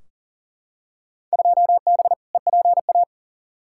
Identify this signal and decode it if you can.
Morse code, heard as a keyed tone, about 40 wpm, 700 Hz. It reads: 2B EPA
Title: Field Day Exchanges Encore - 40wpm